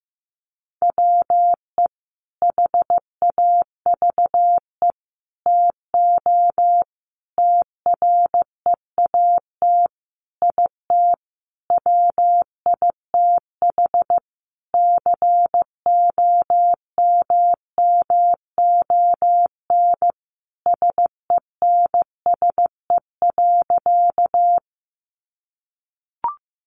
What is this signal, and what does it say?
Morse code, heard as a keyed tone, about 15 wpm, 700 Hz. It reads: WE HAVE TO TREAT IT WITH COMMON SENSE.
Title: Sentences from Top 1300 Words 15wpm